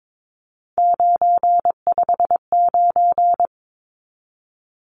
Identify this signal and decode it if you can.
Morse code, heard as a keyed tone, about 22 wpm, 700 Hz. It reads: 959
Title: Numbers - 3-Digit 22wpm